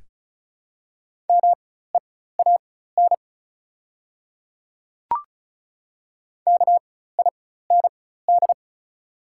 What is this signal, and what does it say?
Morse code, heard as a keyed tone, about 36 wpm, spaced as for 16 wpm, 700 Hz. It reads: MEAN KIND
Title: 4x Character Spacing - Top 300 Words - 35wpm